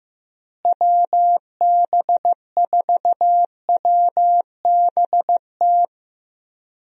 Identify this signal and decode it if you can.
Morse code, heard as a keyed tone, about 15 wpm, 700 Hz. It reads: WB4WBT
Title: USA Call Signs - 6 Characters - Repeated 3x - 15wpm